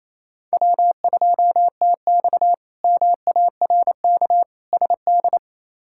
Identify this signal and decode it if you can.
Morse code, heard as a keyed tone, about 27 wpm, 700 Hz. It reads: W2TX MARK SD
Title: K1USN SST QSO Practice - 28wpm